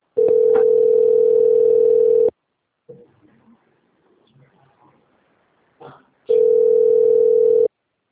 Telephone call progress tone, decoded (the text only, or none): ringback tone